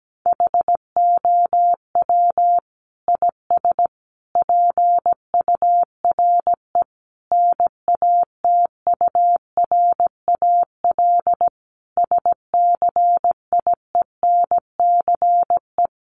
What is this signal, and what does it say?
Morse code, heard as a keyed tone, about 17 wpm, 700 Hz. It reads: HOW IS PURE NATURAL SCIENCE
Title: Sentences from Top 1500 Words 17wpm